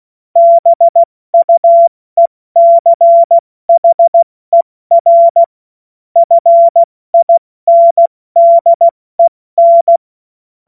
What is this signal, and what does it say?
Morse code, heard as a keyed tone, about 16 wpm, 680 Hz. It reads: BUECHER FINDEN
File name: Morsekurs / Witze / Witz 09 Januar 2024 70 F80